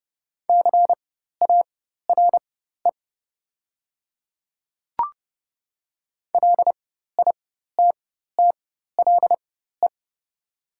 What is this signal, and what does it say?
Morse code, heard as a keyed tone, about 30 wpm, spaced as for 14 wpm, 700 Hz. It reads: CARE LITTLE
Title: 4x Character Spacing - Top 300 Words - 30wpm